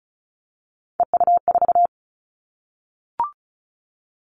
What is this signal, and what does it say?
Morse code, heard as a keyed tone, about 34 wpm, 700 Hz. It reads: EU4
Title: Callsign Prefixes for Top 100 DXCC - 35wpm